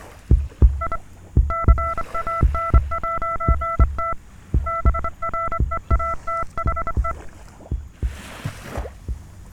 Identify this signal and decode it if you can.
Morse code, heard as a keyed tone, about 24 wpm, 1340 Hz, with noise. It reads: I CAN'T BREATHE